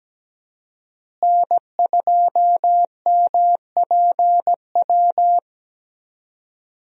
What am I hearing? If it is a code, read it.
Morse code, heard as a keyed tone, about 17 wpm, 700 Hz. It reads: N2MPW
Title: USA Call Signs - 5 Characters - Repeated 3x - 17wpm